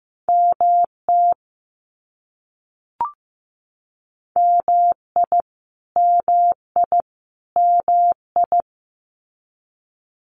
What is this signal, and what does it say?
Morse code, heard as a keyed tone, about 15 wpm, 700 Hz. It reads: MT MI MI MI
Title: US States Repeated 3x 15wpm